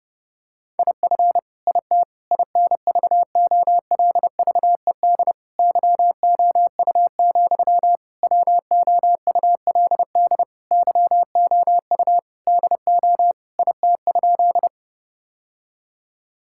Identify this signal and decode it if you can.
Morse code, heard as a keyed tone, about 31 wpm, 700 Hz. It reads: IF IT INVOLVED YOU, WOULD YOU DO IT?
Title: Sentences from Top 1000 Words - Review Part 5 - 30wpm